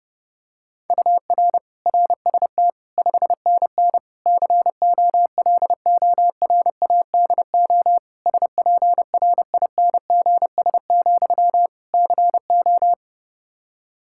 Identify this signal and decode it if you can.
Morse code, heard as a keyed tone, about 30 wpm, 700 Hz. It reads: UR RST 5NN COLORADO SPRINGS, CO
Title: Practice QSO - Exchange 1 - Beginning - 30wpm